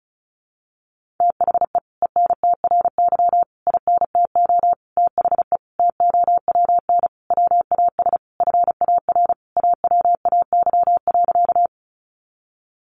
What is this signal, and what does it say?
Morse code, heard as a keyed tone, about 34 wpm, 700 Hz. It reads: THE ENTRY INTO THE TOWN WAS FAR AWAY.
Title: Sentences from Top 1500 Words 35wpm